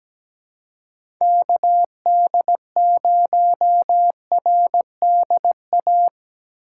Morse code, heard as a keyed tone, about 17 wpm, 700 Hz. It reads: KD0RDA
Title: USA Call Signs - 6 Characters - 2x Character Spacing - 17wpm